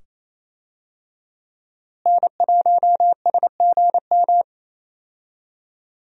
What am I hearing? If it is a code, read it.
Morse code, heard as a keyed tone, about 27 wpm, 700 Hz. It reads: N1SGM
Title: USA Call Signs - 5 Characters - Repeated 2x - 28wpm